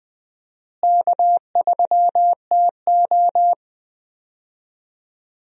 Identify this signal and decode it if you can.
Morse code, heard as a keyed tone, about 21 wpm, 700 Hz. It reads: K3TO
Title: USA Call Signs - 4 Characters - Repeated 3x - 20wpm